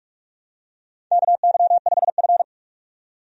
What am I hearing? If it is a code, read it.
Morse code, heard as a keyed tone, about 47 wpm, 700 Hz. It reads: KY5F
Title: USA Call Signs - 4 Characters - 45wpm